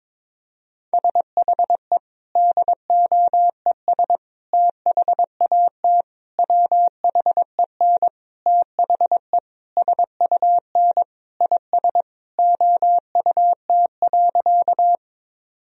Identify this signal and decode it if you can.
Morse code, heard as a keyed tone, about 22 wpm, 700 Hz. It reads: SHE DOES THAT WHEN THE SUN IS OUT.
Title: Sentences from Top 1000 Words - Review Part 1 - 22wpm